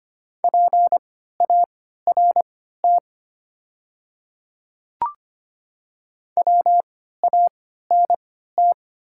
Morse code, heard as a keyed tone, about 25 wpm, spaced as for 14 wpm, 700 Hz. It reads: PART WANT
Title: 3x Character Spacing - Top 300 Words - 25wpm